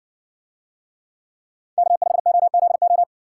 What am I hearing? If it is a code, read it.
Morse code, heard as a keyed tone, about 57 wpm, 700 Hz. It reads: K5OZO